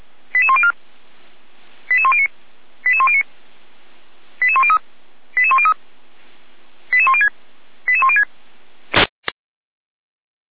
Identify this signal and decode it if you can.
Five-tone selective call (ZVEI), heard as quick runs of five tones, tones of about 70 ms. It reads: 88104 88109 88109 88103 88103 88106 88106
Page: /455/SDR/ZVEI/